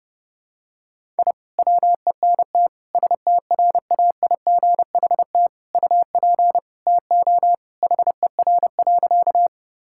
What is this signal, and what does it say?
Morse code, heard as a keyed tone, about 32 wpm, 700 Hz. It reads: I WENT STRAIGHT UP TO HER.
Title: Sentences from Top 1000 Words - Review Part 6 - 30wpm